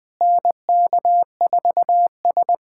Morse code, heard as a keyed tone, about 20 wpm, 700 Hz. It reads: NK4S
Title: USA Callsign DE Callsign 20wpm